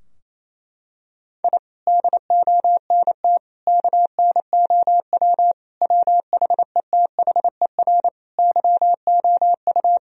Morse code, heard as a keyed tone, about 28 wpm, 700 Hz. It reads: I DONT KNOW WHETHER YOU
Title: Sentences from Top 1000 Words - Review Part 3 - 28wpm